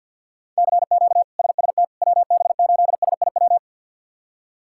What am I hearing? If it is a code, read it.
Morse code, heard as a keyed tone, about 51 wpm, 700 Hz. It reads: CQ SST WB8SIW
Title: K1USN SST QSO Practice - 50wpm